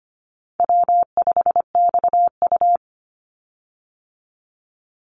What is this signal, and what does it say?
Morse code, heard as a keyed tone, about 25 wpm, 700 Hz. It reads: W5XU
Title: Easy Call Signs Repeated 2x - 25wpm